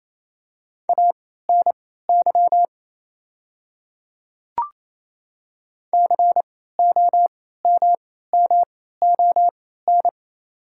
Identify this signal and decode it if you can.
Morse code, heard as a keyed tone, about 28 wpm, spaced as for 16 wpm, 700 Hz. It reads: ANY COMMON
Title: 3x Character Spacing - Top 500 Words - 28wpm